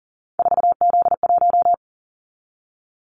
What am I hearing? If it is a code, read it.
Morse code, heard as a keyed tone, about 40 wpm, 700 Hz. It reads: 4Z1
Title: Callsign Prefixes for Top 50 DXCC - 40wpm